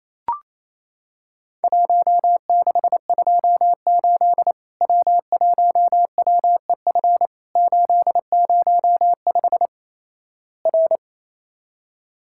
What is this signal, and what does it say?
Morse code, heard as a keyed tone, about 28 wpm, 700 Hz. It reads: 1628 W1WEF 805 R
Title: Worked All Europe DX Contest - QTC Exchange - 28wpm